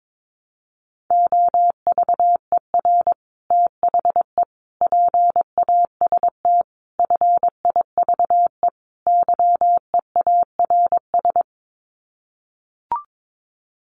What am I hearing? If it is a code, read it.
Morse code, heard as a keyed tone, about 22 wpm, 700 Hz. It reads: OVER THE PAST FIVE YEARS
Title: Sets of 5 Words Encore 22wpm